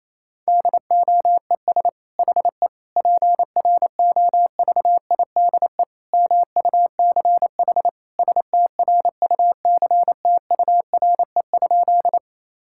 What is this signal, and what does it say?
Morse code, heard as a keyed tone, about 28 wpm, 700 Hz. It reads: DOES HE PROVIDE MUCH STRUCTURE?
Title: Sentences from Top 1000 Words - Review Part 6 - 28wpm